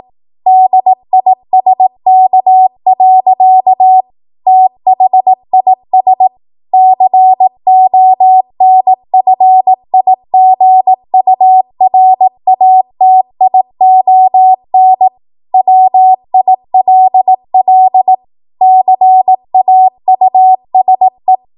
Morse code, heard as a keyed tone, about 18 wpm, 750 Hz. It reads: DISK. THIS CONFIGURATION WILL CAUSE
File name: Archive / 18 WPM / 18WPM